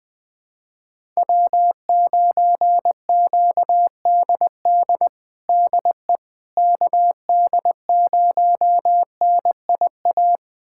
Morse code, heard as a keyed tone, about 20 wpm, 700 Hz. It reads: W9QDD DE KD0NIA
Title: USA Callsign DE Callsign 20wpm